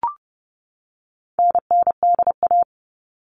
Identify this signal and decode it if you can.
Morse code, heard as a keyed tone, about 30 wpm, 700 Hz. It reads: NNDA
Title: All Cut Numbers - 4-Digit 30wpm